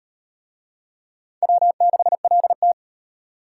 Morse code, heard as a keyed tone, about 34 wpm, 700 Hz. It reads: W6LT